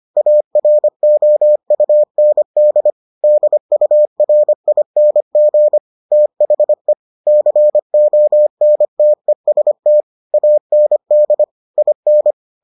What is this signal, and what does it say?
Morse code, heard as a keyed tone, about 25 wpm, 600 Hz. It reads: AROUND DURING THE CONTEST AND IN